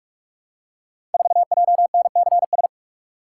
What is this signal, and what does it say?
Morse code, heard as a keyed tone, about 44 wpm, 700 Hz. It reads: 4JNCS
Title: Code Groups - 5 Letter:Numbers - 45wpm